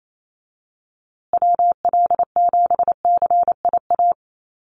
Morse code, heard as a keyed tone, about 28 wpm, 700 Hz. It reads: WL7CIA
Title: USA Call Signs Repeated 2x 28wpm